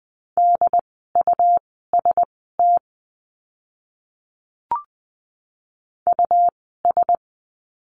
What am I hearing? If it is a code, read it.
Morse code, heard as a keyed tone, about 20 wpm, spaced as for 15 wpm, 700 Hz. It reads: DUST US
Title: 101 Lesson 17f - Letter u - Words 20wpm